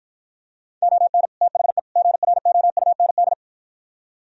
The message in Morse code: ON THE GROUND

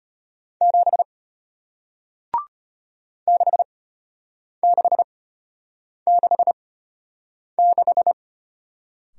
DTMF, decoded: none

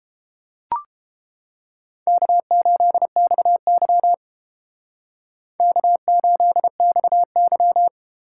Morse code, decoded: K8XY K8XY

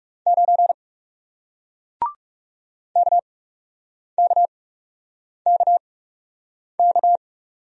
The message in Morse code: 9 KKKK

